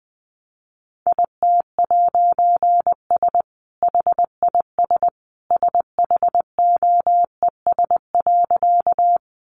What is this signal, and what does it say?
Morse code, heard as a keyed tone, about 20 wpm, 700 Hz. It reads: IT'S HIS SHOES.
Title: A Sound Of Thunder 20wpm